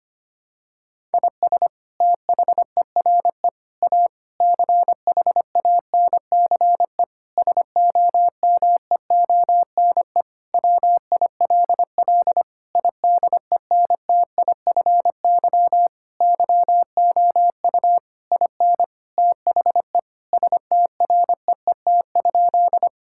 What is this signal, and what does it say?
Morse code, heard as a keyed tone, about 25 wpm, 700 Hz. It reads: IS THERE A CHANCE SOMEONE WILL IDENTIFY YOU IN THE STREET?
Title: Sentences from Top 1000 Words - Review Part 6 - 25wpm